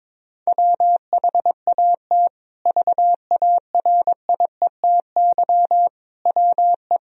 Morse code, heard as a keyed tone, about 22 wpm, 700 Hz. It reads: WHAT VARIETY WE
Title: Sentences from Top 1000 Words - Review Part 4 - 22wpm